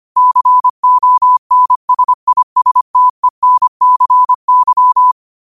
Morse code, heard as a keyed tone, about 25 wpm, 1000 Hz. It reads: CONSISTENCY
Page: New quotes every day in morse code at 25 Words per minute.